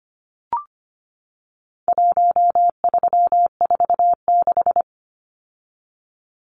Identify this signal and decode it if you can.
Morse code, heard as a keyed tone, about 25 wpm, 700 Hz. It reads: 1346
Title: Numbers - Contest Style - 4-Digit 25wpm